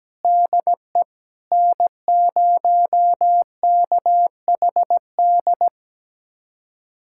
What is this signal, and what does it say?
Morse code, heard as a keyed tone, about 17 wpm, 700 Hz. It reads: DE N0KHD
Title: USA Callsign DE Callsign - 5 Characters - 17wpm